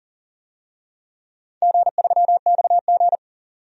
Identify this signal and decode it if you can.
Morse code, heard as a keyed tone, about 40 wpm, 700 Hz. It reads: G3XG
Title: DXCC-100 Call Signs - 4 Characters - 40wpm